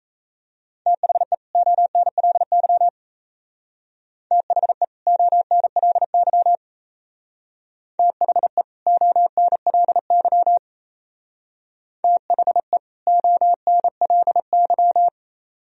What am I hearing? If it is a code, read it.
Morse code, heard as a keyed tone, about 34 wpm, 700 Hz. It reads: THE ONLY THE ONLY THE ONLY THE ONLY
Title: Speed-Racing - Sets of 2 Words 28wpm